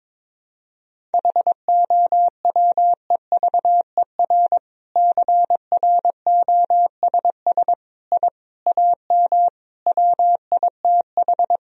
Morse code, heard as a keyed tone, about 22 wpm, 700 Hz. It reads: HOWEVER CROSS I AM WITH
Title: Sentences from Top 1000 Words - Review Part 1 - 22wpm